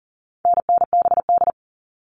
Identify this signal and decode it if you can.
Morse code, heard as a keyed tone, about 40 wpm, 700 Hz. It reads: NNBD
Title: All Cut Numbers - 4-Digit 40wpm